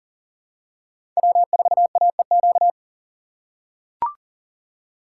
Morse code, heard as a keyed tone, about 40 wpm, 700 Hz. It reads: W4AEQ
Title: USA Call Signs - 5 Characters - 4x Character Spacing - 40wpm